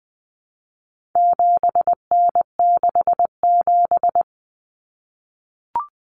Morse code, heard as a keyed tone, about 20 wpm, 700 Hz. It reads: 7N67
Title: Common Cut Numbers - 4-Digit 20wpm